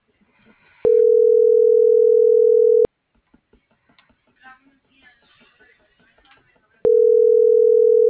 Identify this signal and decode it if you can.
Ringback tone (North America), 440 + 480 Hz.